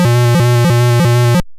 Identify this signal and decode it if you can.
There is no code.